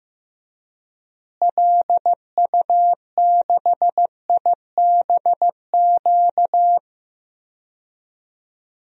Morse code, heard as a keyed tone, about 15 wpm, 700 Hz. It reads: LU6IBQ
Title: DXCC-100 Call Signs - 6 Characters - Repeated 2x - 15wpm